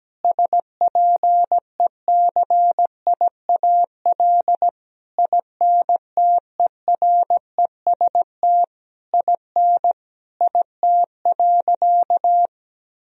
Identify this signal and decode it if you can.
Morse code, heard as a keyed tone, about 17 wpm, 700 Hz. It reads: SPECIAL INTEREST IN IT.
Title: Sentences from Top 1000 Words - Review Part 4 - 17wpm